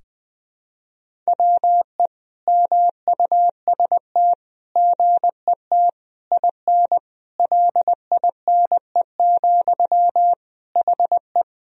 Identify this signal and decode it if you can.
Morse code, heard as a keyed tone, about 20 wpm, 700 Hz. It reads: WE MUST GET IN LINE, HE
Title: Sentences from Top 1000 Words - Review Part 1 - 20wpm